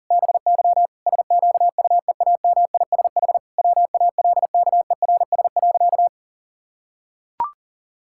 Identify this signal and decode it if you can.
Morse code, heard as a keyed tone, about 37 wpm, 700 Hz. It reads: BY SQUEAMISH WALKERS.
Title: Pangram Challenge v2 - 40wpm